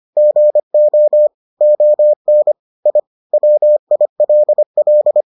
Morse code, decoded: GO ON I WILL

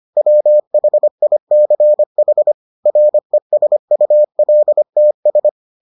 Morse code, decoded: WHICH RESULTS

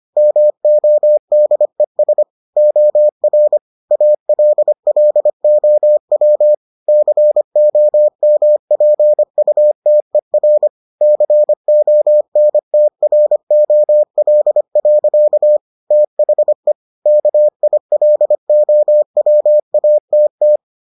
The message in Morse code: MODES OR ALLOW COMPUTER CONTROL. THE KILOWATT

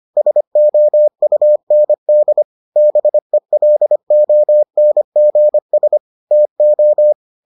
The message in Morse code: SOUND BELONGS TO

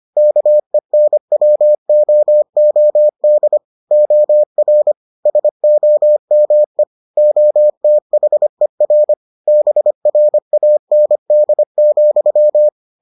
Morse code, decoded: KENWOOD OR SOME OTHER BRAND,

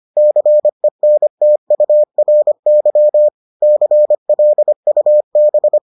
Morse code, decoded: CENTURY CLUB

